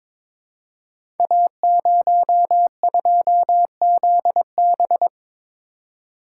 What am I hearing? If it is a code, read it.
Morse code, heard as a keyed tone, about 23 wpm, 700 Hz. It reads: A02ZB
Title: Code Groups - 5 Letter:Numbers - 22wpm